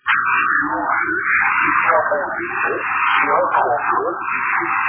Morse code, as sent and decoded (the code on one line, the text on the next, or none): none